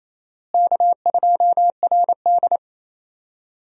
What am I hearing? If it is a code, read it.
Morse code, heard as a keyed tone, about 28 wpm, 700 Hz. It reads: K2RD
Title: USA Call Signs - 4 Characters - Repeated 3x - 28wpm